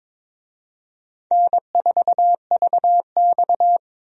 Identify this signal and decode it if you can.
Morse code, heard as a keyed tone, about 22 wpm, 700 Hz. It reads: N4VX
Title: USA Call Signs - 4 Characters - 22wpm